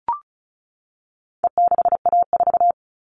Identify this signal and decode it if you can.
Morse code, heard as a keyed tone, about 34 wpm, 700 Hz. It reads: E6A4
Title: All Cut Numbers - 4-Digit 35wpm